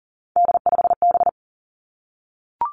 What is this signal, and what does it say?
Morse code, heard as a keyed tone, about 42 wpm, 700 Hz. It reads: D5B
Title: Three Challenging Letter-Number Combinations 40wpm